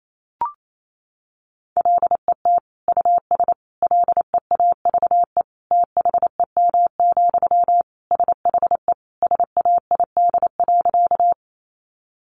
Morse code, heard as a keyed tone, about 28 wpm, 700 Hz. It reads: LET US LEAVE THEM, SHE SAID.
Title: Sentences from Top 400 Words 28wpm